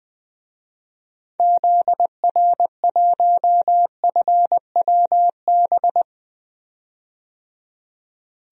Morse code, heard as a keyed tone, about 20 wpm, 700 Hz. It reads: ZR1FWB
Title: DXCC-100 Call Signs - 6 Characters - 20wpm